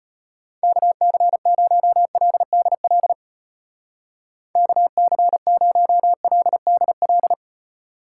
Morse code, heard as a keyed tone, about 31 wpm, 700 Hz. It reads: KC0LDL KC0LDL
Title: Speed Racing - USA Call Signs 25wpm